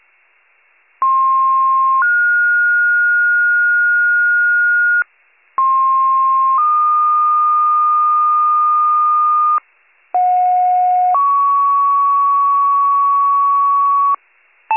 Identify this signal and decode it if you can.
Two-tone page, heard as a one-second tone, then a three-second tone, about 1060 then 1430 Hz, about 1060 then 1220 Hz, about 730 then 1120 Hz. It is heard from 1.0 to 14.1 s.